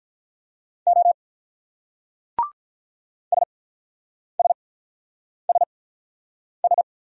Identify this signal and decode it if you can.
Morse code, heard as a keyed tone, about 45 wpm, spaced as for 9 wpm, 700 Hz. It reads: O SSSS